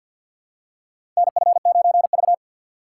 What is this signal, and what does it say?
Morse code, heard as a keyed tone, about 52 wpm, 700 Hz. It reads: NF9V
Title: USA Call Signs - 4 Characters - 2x Character Spacing - 50wpm